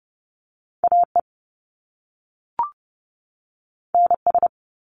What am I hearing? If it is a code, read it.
Morse code, heard as a keyed tone, about 30 wpm, 700 Hz. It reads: AE NS
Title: Top Two Letter Combinations in Words 30wpm